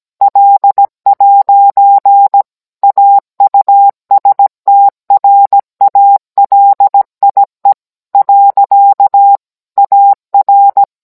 Morse code, heard as a keyed tone, about 17 wpm, 800 Hz. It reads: L'AUSTRALIE. AR